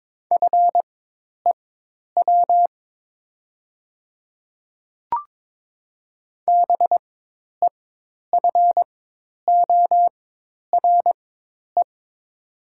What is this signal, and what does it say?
Morse code, heard as a keyed tone, about 22 wpm, spaced as for 10 wpm, 700 Hz. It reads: FEW BEFORE
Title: 4x Character Spacing - Top 300 Words - 22wpm